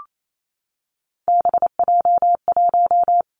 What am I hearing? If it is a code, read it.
Morse code, heard as a keyed tone, about 28 wpm, 700 Hz. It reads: BJ1
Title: Three Challenging Letter-Number Combinations 28wpm